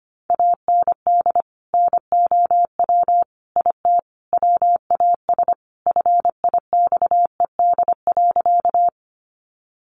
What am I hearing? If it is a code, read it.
Morse code, heard as a keyed tone, about 25 wpm, 700 Hz. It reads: AND NOW IT WAS FIXED.
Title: Sentences from Top 1100 Words 25wpm